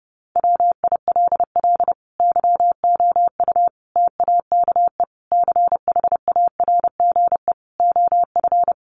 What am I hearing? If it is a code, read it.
Morse code, heard as a keyed tone, about 30 wpm, 700 Hz. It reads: WILL YOU TAKE CHARGE OF
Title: Sentences from Top 900 Words 30wpm